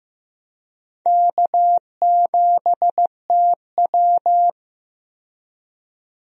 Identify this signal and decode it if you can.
Morse code, heard as a keyed tone, about 15 wpm, 700 Hz. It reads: K7TW
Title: USA Call Signs - 4 Characters - 15wpm